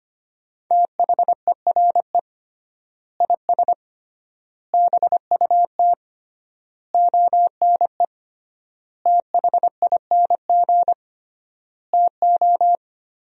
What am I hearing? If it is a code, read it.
Morse code, heard as a keyed tone, about 25 wpm, 700 Hz. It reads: THERE IS BUT ONE THING TO
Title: 3x Word-Spacing - Sentences from Top 100 Words - 25wpm